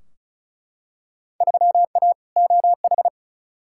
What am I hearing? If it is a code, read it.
Morse code, heard as a keyed tone, about 36 wpm, 700 Hz. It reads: 3A OH
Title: Field Day Exchanges - 35wpm